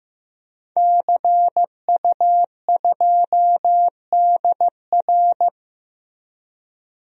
Morse code, heard as a keyed tone, about 15 wpm, 700 Hz. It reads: CU2DR
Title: DXCC-100 Call Signs - 5 Characters - 15wpm